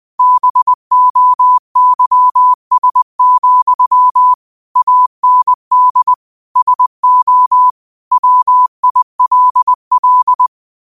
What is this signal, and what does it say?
Morse code, heard as a keyed tone, about 20 wpm, 1000 Hz. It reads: BOYS, AND SO WILL
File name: qotdfiles / 20wpm